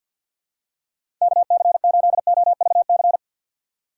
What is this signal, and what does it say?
Morse code, heard as a keyed tone, about 52 wpm, 700 Hz. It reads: KC8OVC
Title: USA Call Signs - 6x Character Spacing - 50wpm